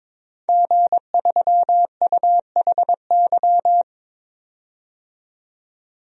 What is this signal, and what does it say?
Morse code, heard as a keyed tone, about 22 wpm, 700 Hz. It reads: G3UHY
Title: DXCC-100 Call Signs - 5 Characters - 22wpm